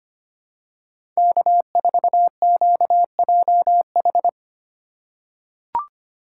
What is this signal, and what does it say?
Morse code, heard as a keyed tone, about 25 wpm, 700 Hz. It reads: K4QJH
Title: USA Call Signs - 5 Characters - 4x Character Spacing - 25wpm